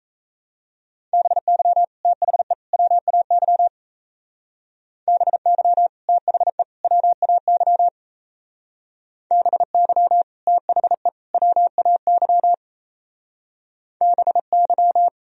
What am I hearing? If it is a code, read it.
Morse code, heard as a keyed tone, about 34 wpm, 700 Hz. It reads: BY THE WAY BY THE WAY BY THE WAY BY
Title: Speed-Racing - Sets of 3 Words 28wpm